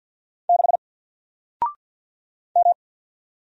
Morse code, heard as a keyed tone, about 48 wpm, 700 Hz. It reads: 6 M
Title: Single Letter-Number Alt 50wpm